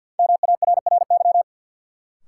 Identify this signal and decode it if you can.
Morse code, heard as a keyed tone, about 51 wpm, 700 Hz. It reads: MARRY
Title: Sentences from Top 1200 Words - New Word Drill - 50wpm